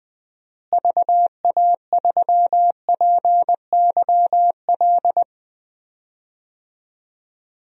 Morse code, heard as a keyed tone, about 21 wpm, 700 Hz. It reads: VA3PYL
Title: DXCC-100 Call Signs - 6 Characters - Repeated 3x - 20wpm